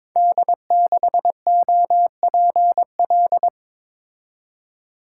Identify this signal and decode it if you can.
Morse code, heard as a keyed tone, about 23 wpm, 700 Hz. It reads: D6OPL
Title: Code Groups - 5 Letter:Numbers - 22wpm